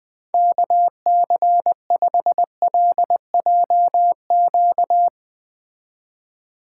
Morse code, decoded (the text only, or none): KC5LJQ